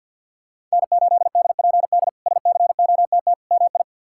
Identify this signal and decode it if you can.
Morse code, heard as a keyed tone, about 51 wpm, 700 Hz. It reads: N8DPD SCOTT MI
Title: K1USN SST QSO Practice - 50wpm